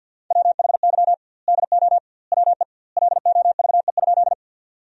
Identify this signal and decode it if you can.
Morse code, heard as a keyed tone, about 52 wpm, 700 Hz. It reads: WHY DO WE LOVE?
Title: 2x Word-Spacing - Sentences from Top 400 Words - 50wpm